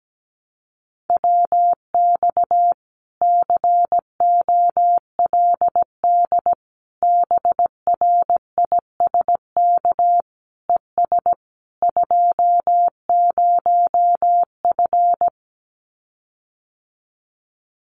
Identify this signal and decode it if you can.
Morse code, heard as a keyed tone, about 17 wpm, 700 Hz. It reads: WX COLD BRISK ES 20F
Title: Common QSO Phrases Encore 17wpm